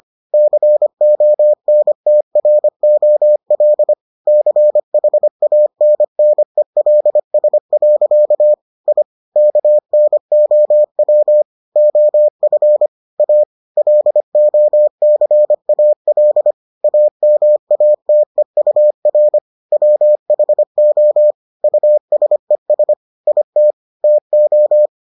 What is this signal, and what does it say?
Morse code, heard as a keyed tone, about 25 wpm, 600 Hz. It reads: CONTROL CHANNELS. I KNOW OF A LOCAL AMATEUR WHO USES IT TO